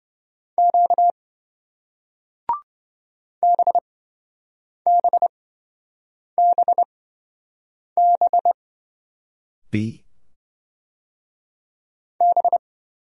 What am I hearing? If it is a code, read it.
Morse code, heard as a keyed tone, about 26 wpm, spaced as for 7 wpm, 700 Hz. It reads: Q BBBB B